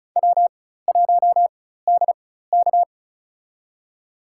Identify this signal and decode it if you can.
Morse code, heard as a keyed tone, about 36 wpm, spaced as for 16 wpm, 700 Hz. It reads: W1DK
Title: USA Call Signs - 4 Characters - 4x Character Spacing - 35wpm